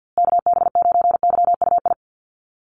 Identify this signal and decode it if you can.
Morse code, heard as a keyed tone, about 50 wpm, 700 Hz. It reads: KB9YUI
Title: USA Call Signs Repeated 3x 50wpm